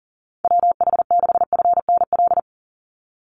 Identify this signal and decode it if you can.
Morse code, heard as a keyed tone, about 38 wpm, 700 Hz. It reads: WH6FNL
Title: USA Call Signs 40wpm